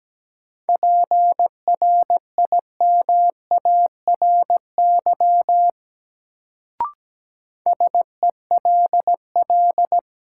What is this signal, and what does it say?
Morse code, heard as a keyed tone, about 17 wpm, 700 Hz. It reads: PRIMARY SELL
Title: Sentences from Top 1100 Words - New Word Drill Alt - 17wpm